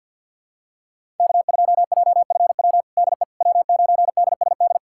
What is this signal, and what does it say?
Morse code, heard as a keyed tone, about 50 wpm, 700 Hz. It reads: K2JFW DE W9DSD
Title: USA Callsign DE Callsign - 5 Characters - 50wpm